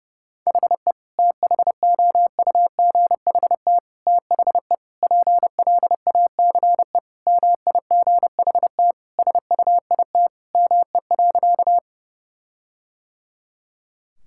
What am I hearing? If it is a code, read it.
Morse code, heard as a keyed tone, about 30 wpm, 700 Hz. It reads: HE THOUGHT THE PLACE MIGHT SUIT ME.
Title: Sentences from Top 1600 Words 30wpm